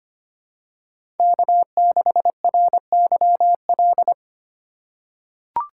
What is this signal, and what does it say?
Morse code, heard as a keyed tone, about 25 wpm, 700 Hz. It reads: K6RYL
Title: USA Call Signs - 5 Characters - 8x Character Spacing - 25wpm